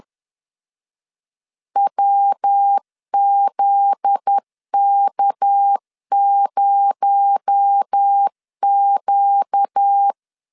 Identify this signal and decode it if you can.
Morse code, heard as a keyed tone, about 10 wpm, 800 Hz. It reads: WZK0Q
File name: LEZIONI / 7 GF9Z / 7 GF9Z 10wpm 2611